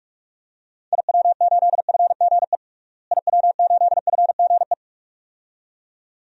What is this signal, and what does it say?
Morse code, heard as a keyed tone, about 47 wpm, 690 Hz. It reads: IW8FGE IW8FGE
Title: DXCC-100 Call Signs - 6 Characters - Repeated 2x - 45wpm